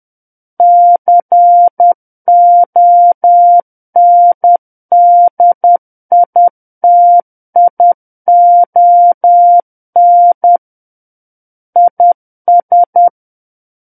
Morse code, heard as a keyed tone, about 10 wpm, 700 Hz. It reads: CONDITION IS
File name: Book One / 01-Chapter 1 - 10 WPM